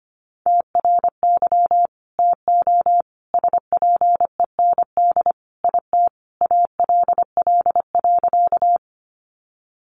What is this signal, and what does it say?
Morse code, heard as a keyed tone, about 25 wpm, 700 Hz. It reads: TRY TO SPEND IT ALL.
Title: Sentences from Top 800 Words 25wpm